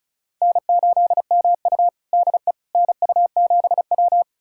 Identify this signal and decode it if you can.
Morse code, heard as a keyed tone, about 36 wpm, 700 Hz. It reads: N8MU DE NU7W
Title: USA Callsign DE Callsign - 4 Characters - 35wpm